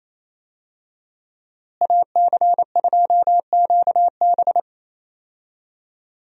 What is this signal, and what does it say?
Morse code, heard as a keyed tone, about 28 wpm, 700 Hz. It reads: AC2QB
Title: USA Call Signs - 5 Characters - Repeated 2x - 28wpm